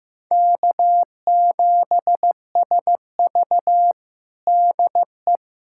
Morse code, heard as a keyed tone, about 16 wpm, 700 Hz. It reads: K7SV DE
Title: Practice QSO - Exchange 4 - Ending - 15wpm